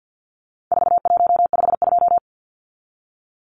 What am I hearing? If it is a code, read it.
Morse code, heard as a keyed tone, about 49 wpm, 700 Hz. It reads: 4152
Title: Numbers - 4-Digit 50wpm